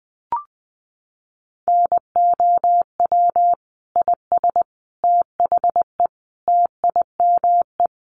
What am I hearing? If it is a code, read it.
Morse code, heard as a keyed tone, about 20 wpm, 700 Hz. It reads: NOW IS THE TIME
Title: Sets of 5 Words Encore2 20wpm